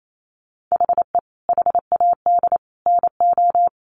Morse code, heard as a keyed tone, about 28 wpm, 700 Hz. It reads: HE HAD NO
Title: Sentences from Top 1200 Words 28wpm